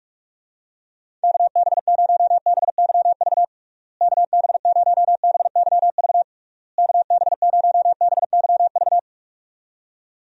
Morse code, heard as a keyed tone, about 46 wpm, 700 Hz. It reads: KB0BYV KB0BYV KB0BYV
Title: USA Call Signs - 6 Characters - Repeated 3x - 45wpm